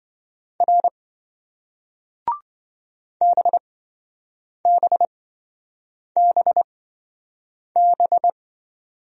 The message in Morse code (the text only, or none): R BBBB